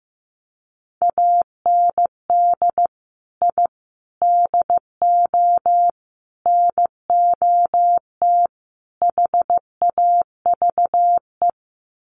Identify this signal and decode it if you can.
Morse code, heard as a keyed tone, about 15 wpm, 700 Hz. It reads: AND I DO NOT HAVE
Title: Sets of 5 Words Encore 15wpm